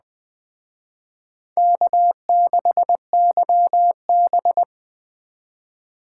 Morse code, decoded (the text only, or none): K6YB